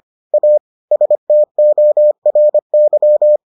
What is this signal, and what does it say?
Morse code, heard as a keyed tone, about 25 wpm, 600 Hz. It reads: A STORY